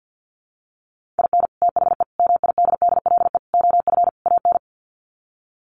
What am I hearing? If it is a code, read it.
Morse code, heard as a keyed tone, about 50 wpm, 700 Hz. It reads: IN THE MIDDLE OF AN
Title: Sets of 5 Words Encore2 50wpm